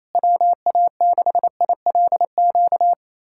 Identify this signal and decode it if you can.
Morse code, heard as a keyed tone, about 28 wpm, 700 Hz. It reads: WA6ILQ
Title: USA Callsign DE Callsign - 6 Characters - 28wpm